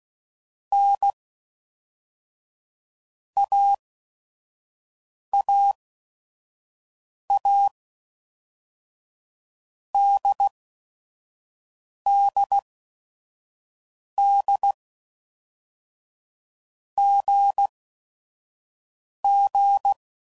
Morse code, heard as a keyed tone, about 16 wpm, 800 Hz. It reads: N A A A D D D G G
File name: Basic Letter Groups By Blocks / Groups@5 / AWUNDG@5 f15